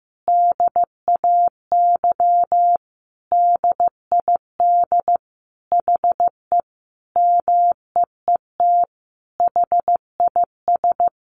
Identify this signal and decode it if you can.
Morse code, heard as a keyed tone, about 15 wpm, 700 Hz. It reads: DAY DID HE MEET HIS
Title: Sentences from Top 1100 Words 15wpm